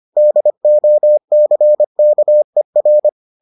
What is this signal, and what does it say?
Morse code, heard as a keyed tone, about 25 wpm, 600 Hz. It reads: DOCKER